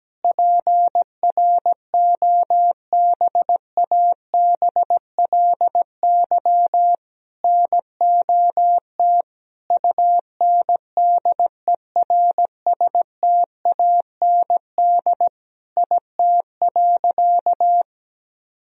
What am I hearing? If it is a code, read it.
Morse code, heard as a keyed tone, about 17 wpm, 700 Hz. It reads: PROBABLY NOT UNDERSTAND IT.
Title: Sentences from Top 1000 Words - Review Part 4 - 17wpm